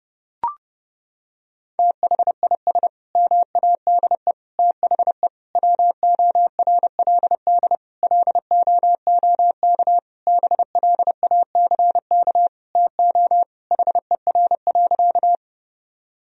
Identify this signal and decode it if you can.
Morse code, heard as a keyed tone, about 30 wpm, 700 Hz. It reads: THIS MADE THE WORLD LOOK BLACK TO HER.
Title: Sentences from Top 1000 Words - Review Part 2 - 30wpm